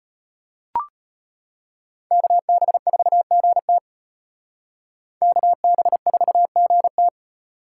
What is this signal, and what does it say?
Morse code, heard as a keyed tone, about 32 wpm, 700 Hz. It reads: KB4GT KB4GT